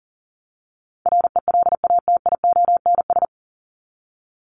Morse code, heard as a keyed tone, about 40 wpm, 700 Hz. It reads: RELATIONS
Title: ES:S Suffix - Top 100 Words with Base Words 40wpm